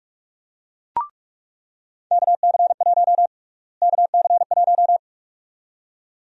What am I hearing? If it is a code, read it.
Morse code, heard as a keyed tone, about 42 wpm, 700 Hz. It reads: KC1 KC1
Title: USA Call Sign Prefixes - Repeated 2x - 45wpm